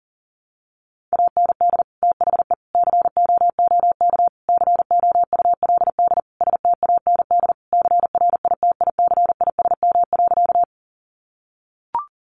Morse code, heard as a keyed tone, about 40 wpm, 700 Hz. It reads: AND THE COOK COULD STAND CRITICISM.
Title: Sentences from Top 1300 Words 40wpm